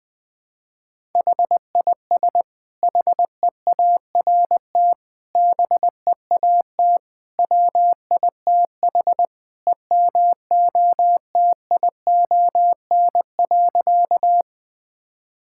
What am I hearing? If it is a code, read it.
Morse code, heard as a keyed tone, about 21 wpm, 700 Hz. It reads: HIS HEART BEAT WITH EMOTION.
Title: Sentences from Top 1600 Words 20wpm